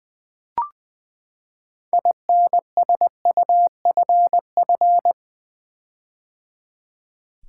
Morse code, heard as a keyed tone, about 20 wpm, 700 Hz. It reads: INSUFF
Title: Build A Word - 6 Letters At A Time - 12-Letter Words - 20wpm